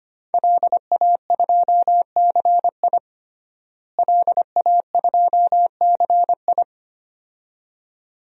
Morse code, heard as a keyed tone, about 25 wpm, 700 Hz. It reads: LA2CI LA2CI
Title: DXCC-100 Call Signs - 5 Characters - Repeated 2x - 25wpm